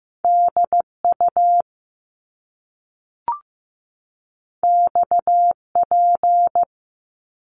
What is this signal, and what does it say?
Morse code, heard as a keyed tone, about 15 wpm, 700 Hz. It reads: DU XP
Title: Top Two Letter Combinations in Words 15wpm